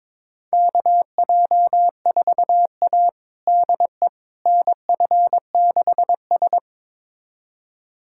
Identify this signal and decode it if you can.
Morse code, heard as a keyed tone, about 22 wpm, 700 Hz. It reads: KJ4A DE NF6S
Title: USA Callsign DE Callsign - 4 Characters - 22wpm